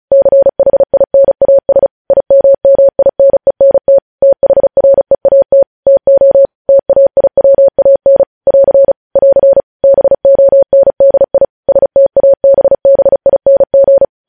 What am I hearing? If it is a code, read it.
Morse code, heard as a keyed tone, about 36 wpm, 550 Hz. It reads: CHINAS IMMINENT THREAT TO TAIWAN <AR> <AR> BONDI STABBING
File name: podcast / News headlines-1x-35x35